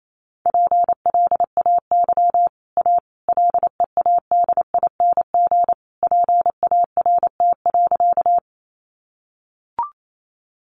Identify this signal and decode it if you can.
Morse code, heard as a keyed tone, about 28 wpm, 700 Hz. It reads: PLAY A LEADING PART.
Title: Sentences from Top 1100 Words 28wpm